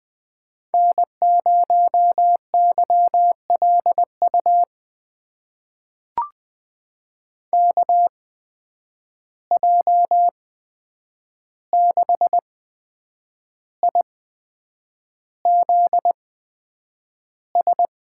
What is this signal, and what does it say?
Morse code, heard as a keyed tone, about 20 wpm, spaced as for 8 wpm, 700 Hz. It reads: N0YLU KJ6IZS
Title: USA Call Signs - 8x Character Spacing - 20wpm